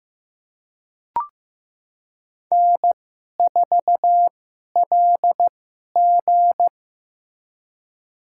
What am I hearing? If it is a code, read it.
Morse code, heard as a keyed tone, about 15 wpm, spaced as for 11 wpm, 700 Hz. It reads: N4LG
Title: USA Call Signs - 4 Characters - 2x Character Spacing - 15wpm